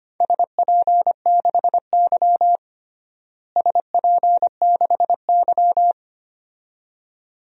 Morse code, heard as a keyed tone, about 25 wpm, 700 Hz. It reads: SP6Y SP6Y
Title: DXCC-100 Call Signs - 4 Characters - Repeated 2x - 25wpm